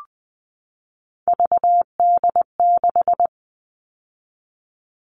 Morse code, heard as a keyed tone, about 20 wpm, 700 Hz. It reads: VD6
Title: All Cut Numbers - 3-Digit 20wpm